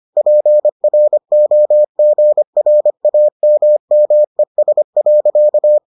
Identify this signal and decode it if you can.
Morse code, heard as a keyed tone, about 25 wpm, 600 Hz. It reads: PROGRAMMES.